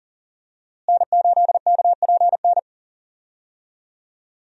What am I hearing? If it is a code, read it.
Morse code, heard as a keyed tone, about 40 wpm, 700 Hz. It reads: N8KPN
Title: USA Call Signs - 5 Characters - Repeated 3x - 40wpm